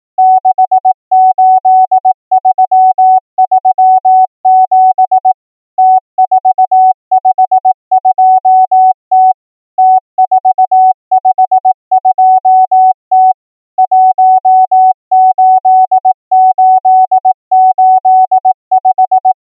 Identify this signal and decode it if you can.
Morse code, heard as a keyed tone, about 18 wpm, 750 Hz. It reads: 68337 T452T T452T 18885
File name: numbers / V2-morse